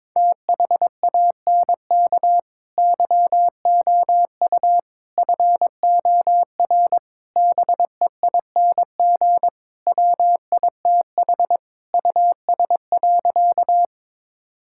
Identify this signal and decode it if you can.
Morse code, heard as a keyed tone, about 22 wpm, 700 Hz. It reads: THANK YOU FOR BEING WITH US.
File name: practice / 22wpm / Sentences with 4-Word Phrases - 22wpm